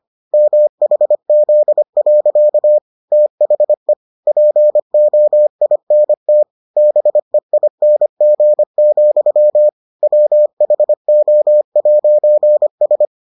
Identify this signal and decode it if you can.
Morse code, heard as a keyed tone, about 25 wpm, 600 Hz. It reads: MHZ. THE POINT BEING, WHO'S